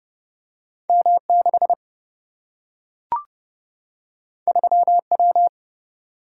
Morse code, heard as a keyed tone, about 29 wpm, 700 Hz. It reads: M 6 3 W
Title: Single-Letter Single-Number 30wpm